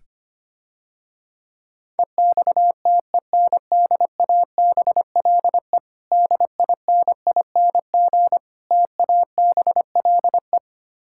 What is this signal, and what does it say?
Morse code, heard as a keyed tone, about 25 wpm, 700 Hz. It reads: EXTENDABLE DINING TABLE
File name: practice / 25wpm / Difficult Words - Speed-Racing with Hints - 25wpm